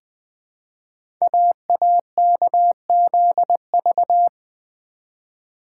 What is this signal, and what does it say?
Morse code, heard as a keyed tone, about 20 wpm, 700 Hz. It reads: AAKZV
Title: Code Groups - 5 Letter:Numbers - 20wpm